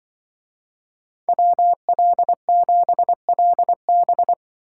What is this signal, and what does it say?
Morse code, heard as a keyed tone, about 24 wpm, 700 Hz. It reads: WL7LB
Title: Speed Racing - USA 2x2 Call Signs 20wpm